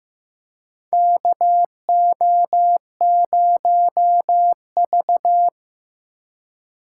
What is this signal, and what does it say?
Morse code, heard as a keyed tone, about 15 wpm, 700 Hz. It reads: KO0V
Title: USA Call Signs - 4 Characters - 3x Character Spacing - 15wpm